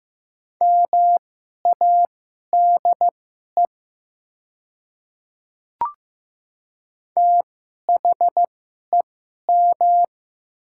Morse code, heard as a keyed tone, about 15 wpm, spaced as for 11 wpm, 700 Hz. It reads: MADE THEM
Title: 2x Character Spacing - Top 200 Words - 15wpm